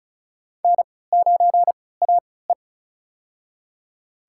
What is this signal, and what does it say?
Morse code, heard as a keyed tone, about 36 wpm, spaced as for 20 wpm, 700 Hz. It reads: N9AE
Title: USA Call Signs - 4 Characters - 3x Character Spacing - 35wpm